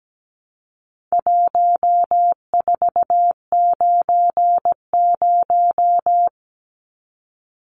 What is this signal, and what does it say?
Morse code, heard as a keyed tone, about 17 wpm, 700 Hz. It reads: 1490
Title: Numbers - 4-Digit 17wpm